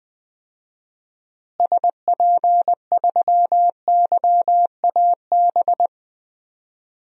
Morse code, heard as a keyed tone, about 20 wpm, 700 Hz. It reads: SP3YAB
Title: DXCC-100 Call Signs - 6 Characters - Repeated 2x - 20wpm